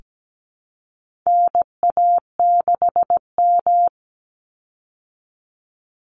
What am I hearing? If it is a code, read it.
Morse code, heard as a keyed tone, about 17 wpm, 700 Hz. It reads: NA6M
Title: Call Signs Repeated 3x - 17wpm